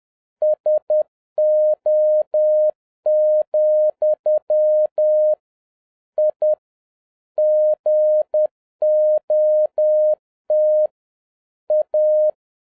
Morse code, heard as a keyed tone, about 10 wpm, 610 Hz. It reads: SO, I GOT A